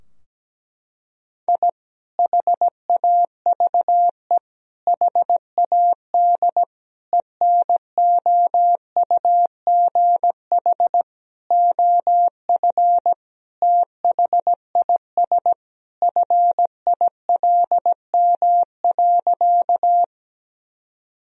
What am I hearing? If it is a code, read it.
Morse code, heard as a keyed tone, about 17 wpm, 700 Hz. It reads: I HAVE HAD ENOUGH OF THIS FILM.
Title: Sentences from Top 1000 Words - Review Part 3 - 17wpm